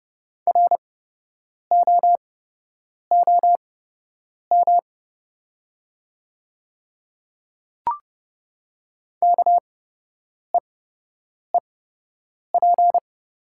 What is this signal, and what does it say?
Morse code, heard as a keyed tone, about 30 wpm, spaced as for 8 wpm, 700 Hz. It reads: ROOM KEEP
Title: 8x Character Spacing - Top 300 Words - 30wpm